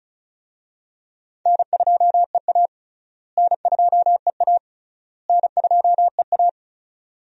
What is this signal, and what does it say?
Morse code, heard as a keyed tone, about 34 wpm, 700 Hz. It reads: N2EA N2EA N2EA
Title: USA Call Signs - 4 Characters - Repeated 3x - 35wpm